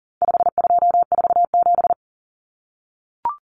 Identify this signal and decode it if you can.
Morse code, heard as a keyed tone, about 40 wpm, 700 Hz. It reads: 5247
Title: Common Cut Numbers - 4-Digit 40wpm